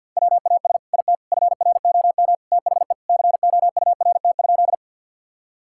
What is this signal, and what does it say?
Morse code, heard as a keyed tone, about 50 wpm, 700 Hz. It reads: WAS IT FROM THE COURT?
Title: Sentences from Top 1600 Words 50wpm